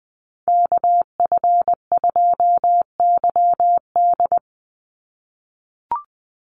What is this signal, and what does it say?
Morse code, heard as a keyed tone, about 20 wpm, 700 Hz. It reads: KF2YD
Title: USA Call Signs 20wpm